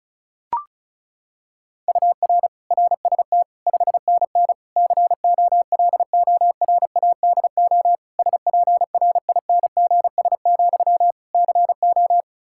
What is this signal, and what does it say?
Morse code, heard as a keyed tone, about 36 wpm, 700 Hz. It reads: UR RST 5NN COLORADO SPRINGS, CO
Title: Practice QSO - Exchange 1 - Beginning - 35wpm